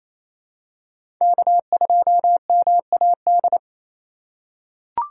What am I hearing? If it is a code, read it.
Morse code, heard as a keyed tone, about 28 wpm, 700 Hz. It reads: K2MAD
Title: USA Call Signs - 5 Characters - 4x Character Spacing - 28wpm